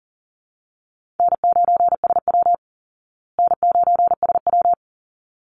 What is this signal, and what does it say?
Morse code, heard as a keyed tone, about 40 wpm, 700 Hz. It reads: N9SW N9SW
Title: Easy Call Signs Repeated 2x - 40wpm